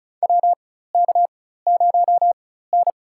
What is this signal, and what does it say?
Morse code, heard as a keyed tone, about 36 wpm, spaced as for 16 wpm, 700 Hz. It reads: WK0N
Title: USA Call Signs - 4 Characters - 4x Character Spacing - 35wpm